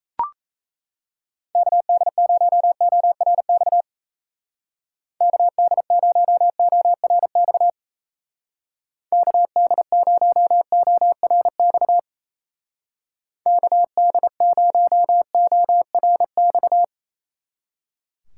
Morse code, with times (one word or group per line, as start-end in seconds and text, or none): 1.6-3.8 KD0ORX
5.2-7.7 KD0ORX
9.1-12.0 KD0ORX
13.5-16.8 KD0ORX